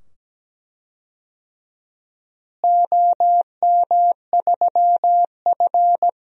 Morse code, heard as a keyed tone, about 17 wpm, 700 Hz. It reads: OM3F
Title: DXCC-100 Call Signs - 4 Characters - Repeated 2x - 17wpm